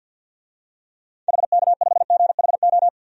Morse code, heard as a keyed tone, about 51 wpm, 700 Hz. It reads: HK5GHO
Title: DXCC-100 Call Signs - 6 Characters - 50wpm